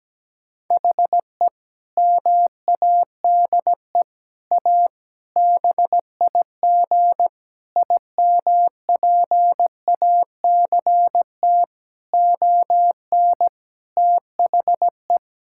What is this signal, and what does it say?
Morse code, heard as a keyed tone, about 17 wpm, 700 Hz. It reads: HE MADE A BIG IMPACT ON THE
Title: Sentences from Top 1000 Words - Review Part 6 - 17wpm